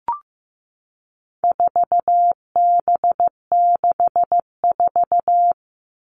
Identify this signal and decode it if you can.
Morse code, heard as a keyed tone, about 15 wpm, 700 Hz. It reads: 4B64
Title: All Cut Numbers - 4-Digit 15wpm